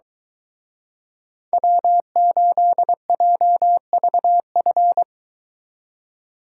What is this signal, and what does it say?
Morse code, heard as a keyed tone, about 23 wpm, 700 Hz. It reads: W8JVF